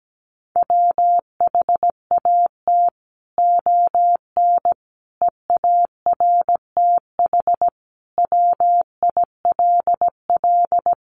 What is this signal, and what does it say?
Morse code, heard as a keyed tone, about 17 wpm, 700 Hz. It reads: WHAT ON EARTH WILL
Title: Sentences from Top 1100 Words 17wpm